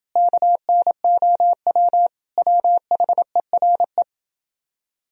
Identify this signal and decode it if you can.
Morse code, heard as a keyed tone, about 30 wpm, 700 Hz. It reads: KNOW WHERE